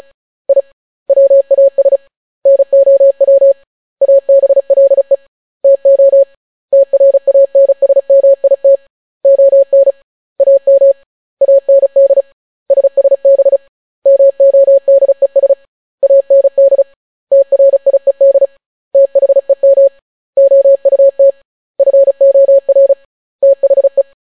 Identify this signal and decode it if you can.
Morse code, heard as a keyed tone, about 35 wpm, 550 Hz. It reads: I WAS NOW ABLE TO TRANSMIT ON AM AND SSB MODES AND TRIED THEM OUT FOR THE